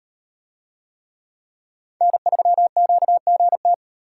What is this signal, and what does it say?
Morse code, heard as a keyed tone, about 37 wpm, 700 Hz. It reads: N3QGT